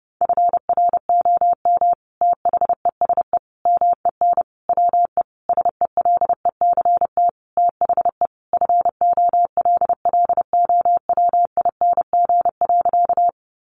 Morse code, heard as a keyed tone, about 30 wpm, 700 Hz. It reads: FROM THESE MEN WE SELECT THE FOLLOWING.
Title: Sentences from Top 1300 Words 30wpm